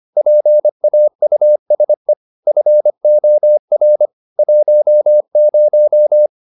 Morse code, heard as a keyed tone, about 25 wpm, 600 Hz. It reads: PAUSE FOR 10
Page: Ferie 25 wpm